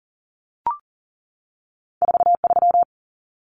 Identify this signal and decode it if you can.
Morse code, heard as a keyed tone, about 40 wpm, 700 Hz. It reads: 43
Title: Common Cut Numbers - 2-Digit 40wpm